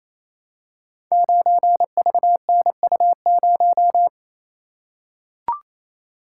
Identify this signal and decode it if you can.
Morse code, heard as a keyed tone, about 28 wpm, 690 Hz. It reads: 9VNU0
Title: Code Groups - 5 Letter:Numbers - 28wpm